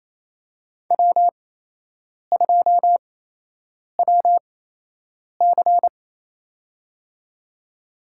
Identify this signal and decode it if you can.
Morse code, heard as a keyed tone, about 28 wpm, spaced as for 8 wpm, 700 Hz. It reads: W2WC
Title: USA Call Signs - 4 Characters - 8x Character Spacing - 28wpm